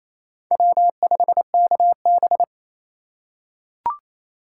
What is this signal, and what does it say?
Morse code, heard as a keyed tone, about 28 wpm, 700 Hz. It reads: W5KB
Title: USA Call Signs - 4 Characters - 4x Character Spacing - 28wpm